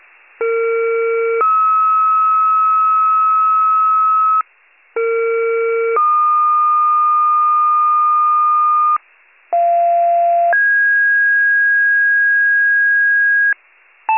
Two-tone page, heard as a one-second tone, then a three-second tone, about 460 then 1290 Hz, about 460 then 1220 Hz, about 690 then 1650 Hz, with noise. It is heard from 0.4 to 13.5 s.